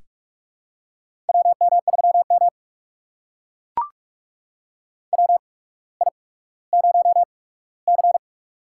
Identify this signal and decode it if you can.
Morse code, heard as a keyed tone, about 44 wpm, spaced as for 19 wpm, 700 Hz. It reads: WM3M WI0C
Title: USA Call Signs - 4 Characters - 8x Character Spacing - 45wpm